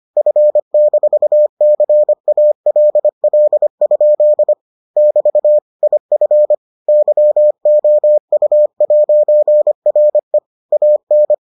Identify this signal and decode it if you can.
Morse code, heard as a keyed tone, about 25 wpm, 600 Hz. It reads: F-CALL? <BT> IF YOU'RE AN